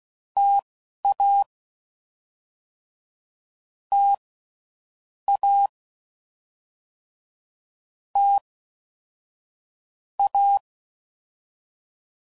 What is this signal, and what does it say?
Morse code, heard as a keyed tone, about 16 wpm, spaced as for 12 wpm, 800 Hz. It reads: TA T A T A
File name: Mixed / Variable Sp / T AOEIs